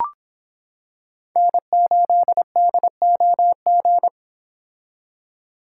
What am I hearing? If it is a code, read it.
Morse code, heard as a keyed tone, about 25 wpm, 700 Hz. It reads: N8DOG